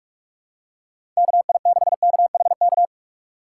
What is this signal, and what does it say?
Morse code, heard as a keyed tone, about 44 wpm, 700 Hz. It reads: KI6KHK
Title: USA Call Signs - 6 Characters - 6x Character Spacing - 45wpm